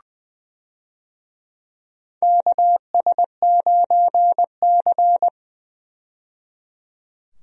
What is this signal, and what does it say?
Morse code, heard as a keyed tone, about 20 wpm, 700 Hz. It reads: KS9C
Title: USA Call Signs - 4 Characters - 20wpm